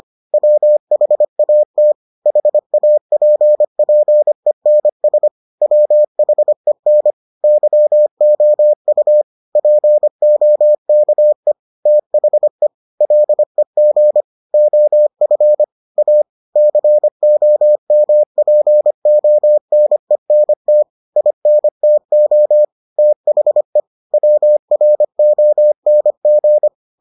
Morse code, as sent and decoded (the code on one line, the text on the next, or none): .-- .... .- - / .... .- .--. .--. -. ... / .-- .... -. / -.-- --- ..- / .--. --- -.- . / - .... / .-.. --. / --- ..-. / .- / -.-. --- -- .--. --- -. -. - / .. -. - --- / - .... / .-- .-. --- -. --.
WHAT HAPPENS WHEN YOU POKE THE LEG OF A COMPONENT INTO THE WRONG